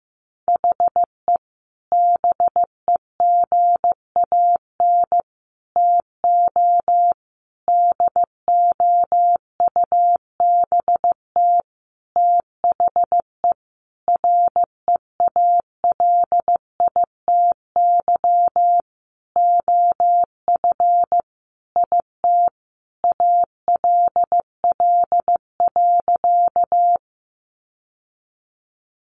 Morse code, heard as a keyed tone, about 15 wpm, 700 Hz. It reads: HE BEGAN TO DOUBT THE REALITY OF IT ALL.
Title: Sentences from Top 1500 Words 15wpm